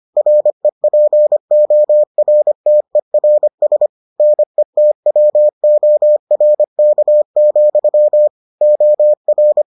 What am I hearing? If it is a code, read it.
Morse code, heard as a keyed tone, about 25 wpm, 600 Hz. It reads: REPORTERS NETWORK, OR